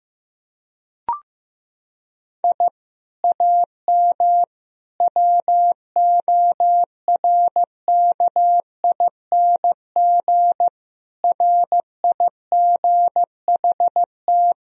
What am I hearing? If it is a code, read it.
Morse code, heard as a keyed tone, about 15 wpm, 700 Hz. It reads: I AM WORKING RIGHT
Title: Sentences from Top 1000 Words - Review Part 4 - 15wpm